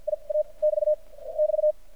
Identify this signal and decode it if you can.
Morse code, heard as a keyed tone, about 58 wpm, 610 Hz. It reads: I A <BT> <BT>
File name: archived / CRSnet